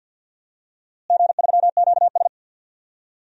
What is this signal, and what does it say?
Morse code, heard as a keyed tone, about 49 wpm, 700 Hz. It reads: G3QS
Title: DXCC-100 Call Signs - 4 Characters - Repeated 2x - 50wpm